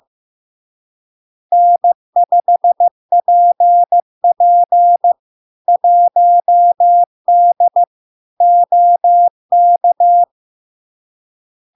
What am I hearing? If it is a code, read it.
Morse code, heard as a keyed tone, about 15 wpm, 700 Hz. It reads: N5PP 1D OK
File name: LARK-FD-2022-Log-in-CW-15wpm-700-Hz